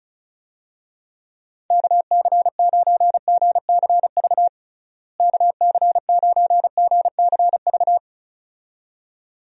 Morse code, heard as a keyed tone, about 34 wpm, 700 Hz. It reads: KC9GCV KC9GCV
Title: USA Call Signs - 6 Characters - Repeated 2x - 35wpm